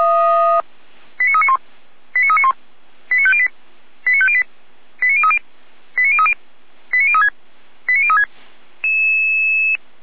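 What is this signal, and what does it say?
Five-tone selective call (ZVEI), heard as quick runs of five tones, tones of about 70 ms. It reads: 89391 89391 89508 89508 89030 89030 89036 89036
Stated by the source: ZVEI five-tone selective call received on an SDR.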